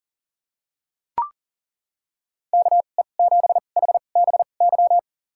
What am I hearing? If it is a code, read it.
Morse code, heard as a keyed tone, about 40 wpm, spaced as for 29 wpm, 700 Hz. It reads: KE7HBY
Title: USA Call Signs - 2x Character Spacing - 40wpm